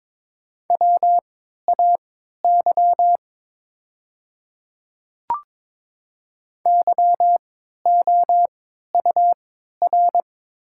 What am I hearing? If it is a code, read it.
Morse code, heard as a keyed tone, about 22 wpm, spaced as for 12 wpm, 700 Hz. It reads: WAY YOUR
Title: 3x Character Spacing - Top 100 Words - 22wpm